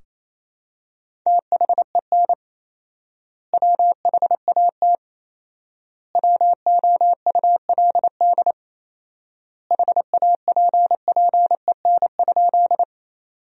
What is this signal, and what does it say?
Morse code, heard as a keyed tone, about 28 wpm, 700 Hz. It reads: THEN WHAT WOULD HAPPEN?
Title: 4x Word-Spacing - Sentences from Top 400 Words - 28wpm